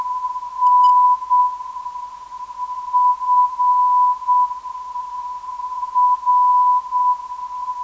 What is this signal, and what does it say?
Morse code, heard as a keyed tone, about 7 wpm, spaced as for 4 wpm, 990 Hz. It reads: NFR
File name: ndb zvok / NFR - 463kHz